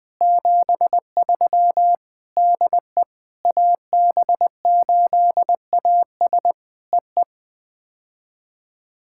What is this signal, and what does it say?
Morse code, heard as a keyed tone, about 19 wpm, 700 Hz. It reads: 73 DE AB8AS EE
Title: POTA QSO Practice - 20wpm